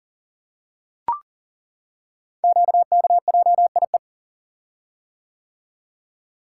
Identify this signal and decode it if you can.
Morse code, heard as a keyed tone, about 42 wpm, 700 Hz. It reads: QKJIE
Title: Code Groups - 5 Letters - 40wpm